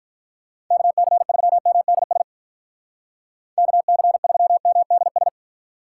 Morse code, heard as a keyed tone, about 44 wpm, 700 Hz. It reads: KC3MDS KC3MDS